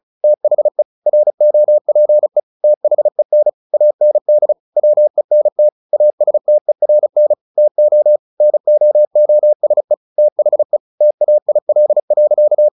Morse code, heard as a keyed tone, about 34 wpm, 600 Hz. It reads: THE ROPE THEN AND WENT ASTERN TO NOOSE THE TAIL.
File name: the old man and the sea 2 35wpm